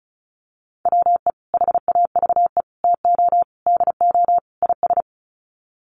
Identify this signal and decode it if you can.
Morse code, heard as a keyed tone, about 34 wpm, 700 Hz. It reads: WE HAVE TO DO IS
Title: Sets of 5 Words Encore 35wpm